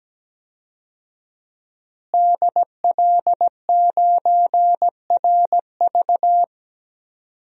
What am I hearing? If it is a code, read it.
Morse code, heard as a keyed tone, about 17 wpm, 700 Hz. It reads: DL9RV
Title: DXCC-100 Call Signs - 5 Characters - Repeated 2x - 17wpm